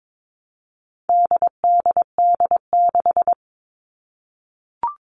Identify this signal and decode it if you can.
Morse code, heard as a keyed tone, about 22 wpm, 700 Hz. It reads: DDD6
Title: All Cut Numbers - 4-Digit 22wpm